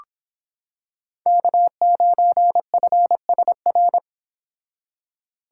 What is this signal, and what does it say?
Morse code, heard as a keyed tone, about 26 wpm, 700 Hz. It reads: K9FSR